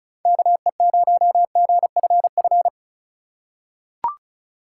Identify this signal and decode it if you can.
Morse code, heard as a keyed tone, about 36 wpm, 700 Hz. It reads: KE0GFF
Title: USA Call Signs - 4x Character Spacing - 35wpm